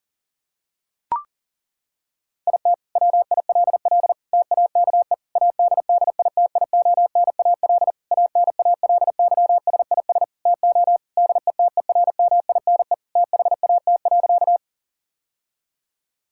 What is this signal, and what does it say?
Morse code, heard as a keyed tone, about 40 wpm, 690 Hz. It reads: IT WILL TAKE ADDITIONAL ANALYSIS TO DETERMINE THAT.
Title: Sentences from Top 1000 Words - Review Part 5 - 40wpm